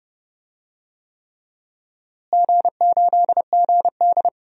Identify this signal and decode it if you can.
Morse code, heard as a keyed tone, about 30 wpm, 700 Hz. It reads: G8GD
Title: DXCC-100 Call Signs - 4 Characters - Repeated 3x - 30wpm